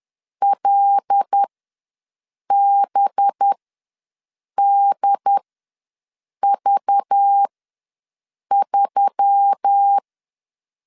Morse code, heard as a keyed tone, about 10 wpm, spaced as for 6 wpm, 800 Hz. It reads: LBDV3
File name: LEZIONI / 5 LBVC / 5 LBV ws 10wpm 3645